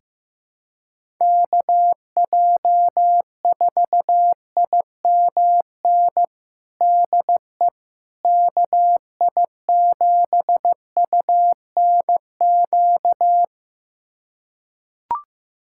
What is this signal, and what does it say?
Morse code, heard as a keyed tone, about 15 wpm, 700 Hz. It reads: KJ4IMN DE KI7UNQ
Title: USA Callsign DE Callsign - 6 Characters - 15wpm